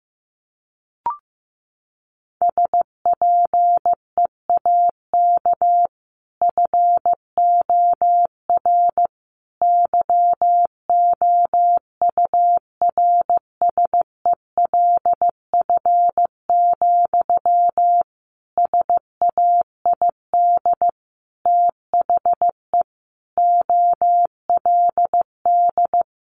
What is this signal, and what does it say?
Morse code, heard as a keyed tone, about 14 wpm, 700 Hz. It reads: SPEAK FOR YOURSELF, SAID THE OLD
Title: Sentences from Top 1100 Words 15wpm